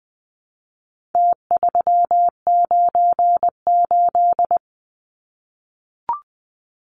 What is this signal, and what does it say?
Morse code, heard as a keyed tone, about 20 wpm, 700 Hz. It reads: T398
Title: Numbers - Contest Style - 4-Digit 20wpm